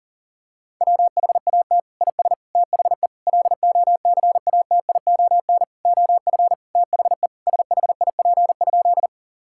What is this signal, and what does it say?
Morse code, heard as a keyed tone, about 40 wpm, 700 Hz. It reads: WHAT IS THE LOCATION OF THE SHIP?
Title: Sentences from Top 1000 Words - Review Part 6 - 40wpm